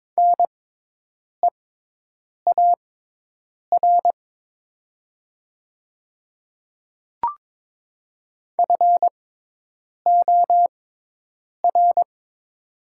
Morse code, heard as a keyed tone, about 22 wpm, spaced as for 8 wpm, 700 Hz. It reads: NEAR FOR
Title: 6x Character Spacing - Top 200 Words - 22wpm